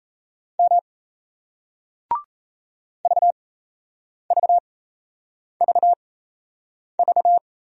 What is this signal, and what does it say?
Morse code, heard as a keyed tone, about 38 wpm, spaced as for 8 wpm, 700 Hz. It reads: M VVVV